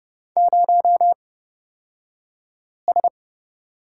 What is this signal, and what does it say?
Morse code, heard as a keyed tone, about 30 wpm, spaced as for 5 wpm, 700 Hz. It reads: 0S
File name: Lesson 38d - Letter q - Two Characters 30wpm